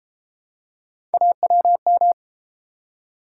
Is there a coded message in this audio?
Yes: Morse code from 1.1 to 2.1 s, heard as a keyed tone, about 33 wpm, 700 Hz. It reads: AWM